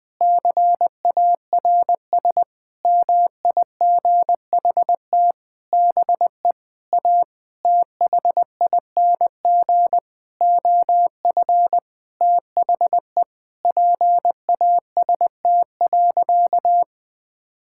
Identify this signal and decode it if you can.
Morse code, heard as a keyed tone, about 20 wpm, 700 Hz. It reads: CARS MIGHT BE A THING OF THE PAST.
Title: Sentences with 6-Word Phrases - Spoken Beginning - 20wpm